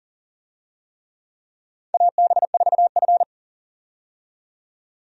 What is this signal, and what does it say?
Morse code, heard as a keyed tone, about 40 wpm, 700 Hz. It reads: AB4F
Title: USA Call Signs - 4 Characters - 40wpm